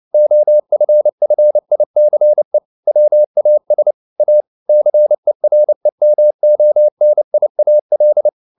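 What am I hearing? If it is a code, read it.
Morse code, heard as a keyed tone, about 29 wpm, 600 Hz. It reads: OFFICE WAS A CEREMONIAL
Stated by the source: Morse code at 29 wpm, sending Ken Follett's novel Never in English, from CW med Gnister.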